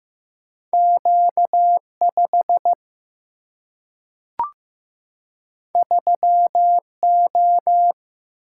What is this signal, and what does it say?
Morse code, heard as a keyed tone, about 15 wpm, 700 Hz. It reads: Q5 3O
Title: Single-Letter Single-Number 15wpm